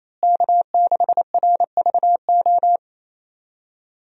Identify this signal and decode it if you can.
Morse code, heard as a keyed tone, about 28 wpm, 700 Hz. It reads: K6RVO
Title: USA Call Signs - 5 Characters - Repeated 3x - 28wpm